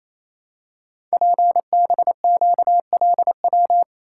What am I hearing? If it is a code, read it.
Morse code, heard as a keyed tone, about 29 wpm, 700 Hz. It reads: PBQLW
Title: Code Groups - 5 Letters - 28wpm